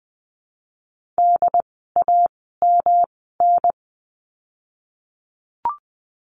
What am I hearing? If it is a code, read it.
Morse code, heard as a keyed tone, about 20 wpm, spaced as for 15 wpm, 700 Hz. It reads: DAMN
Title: 108 Lesson 18f - Letter m - Words 20wpm